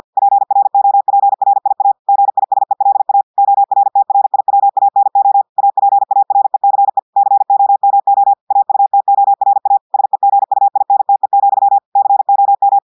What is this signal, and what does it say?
Morse code, heard as a keyed tone, about 50 wpm, 800 Hz. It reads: PROPRIA GISELA ORTRIWANO APARECE COMO AUTORA SEGUINTE, COM
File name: audios / portugais / portugais texte en clair 1 vitesse 50 mots minutes